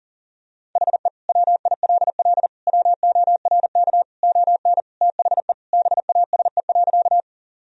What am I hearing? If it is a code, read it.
Morse code, heard as a keyed tone, about 40 wpm, 700 Hz. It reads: HE WILL WORK ON THE BASE.
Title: Sentences from Top 1000 Words - Review Part 1 - 40wpm